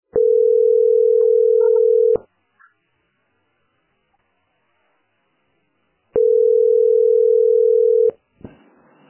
Ringback tone (North America), 440 + 480 Hz.